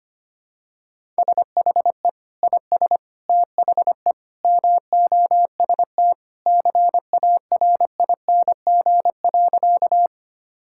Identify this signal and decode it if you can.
Morse code, heard as a keyed tone, about 25 wpm, 690 Hz. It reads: SHE IS THE MOST CARING.
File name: practice / 25wpm / Sentences with 2-Word Phrases - 25wpm